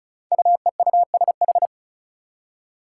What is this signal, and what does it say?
Morse code, heard as a keyed tone, about 36 wpm, 700 Hz. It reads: UEUSH
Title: Five Challenging Letter-Number Combinations 35wpm